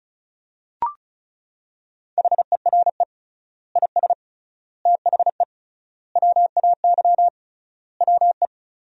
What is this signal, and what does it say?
Morse code, heard as a keyed tone, about 36 wpm, 700 Hz. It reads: HERE IS THE WAY WE
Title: 3x Word-Spacing - Sentences from Top 200 Words - 35wpm